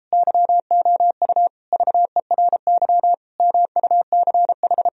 Morse code, heard as a keyed tone, about 37 wpm, 700 Hz. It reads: YOU VERY MUCH